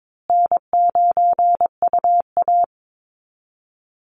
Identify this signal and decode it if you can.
Morse code, heard as a keyed tone, about 22 wpm, 700 Hz. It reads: N9UA
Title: Easy Call Signs Repeated 2x - 22wpm